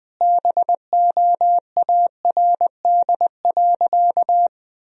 Morse code, heard as a keyed tone, about 20 wpm, 700 Hz. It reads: BOARD.
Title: Sentences from Top 1000 Words - Review Part 4 - 20wpm